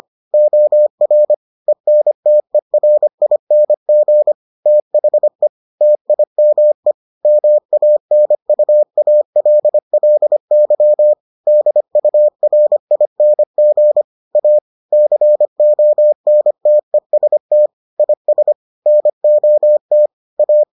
Morse code, heard as a keyed tone, about 25 wpm, 600 Hz. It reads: OR ENTERING THE TIME MANUALLY DURING A CONTEST IS NOT A